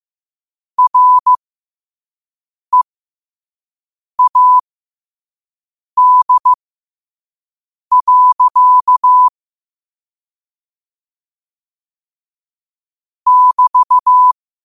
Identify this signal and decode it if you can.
Morse code, heard as a keyed tone, about 15 wpm, spaced as for 5 wpm, 1000 Hz. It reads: READ. <BT>
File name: qotdfiles / 5wpm